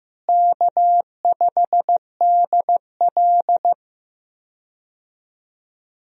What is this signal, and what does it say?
Morse code, heard as a keyed tone, about 15 wpm, 700 Hz. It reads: K5DL
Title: USA Call Signs - 4 Characters - Repeated 3x - 15wpm